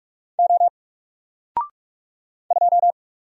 Morse code, heard as a keyed tone, about 45 wpm, 700 Hz. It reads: O 2
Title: Single Letter-Number 45wpm